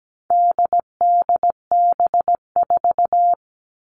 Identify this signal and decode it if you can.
Morse code, heard as a keyed tone, about 17 wpm, 700 Hz. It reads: DDB4
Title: All Cut Numbers - 4-Digit 17wpm